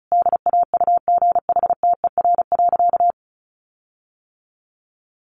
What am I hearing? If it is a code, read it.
Morse code, heard as a keyed tone, about 34 wpm, 700 Hz. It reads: DAUGHTER.
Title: Sentences from Top 1500 Words 35wpm